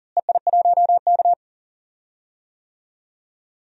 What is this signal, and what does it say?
Morse code, heard as a keyed tone, about 40 wpm, 700 Hz. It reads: EI1K
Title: DXCC-100 Call Signs - 4 Characters - Repeated 2x - 40wpm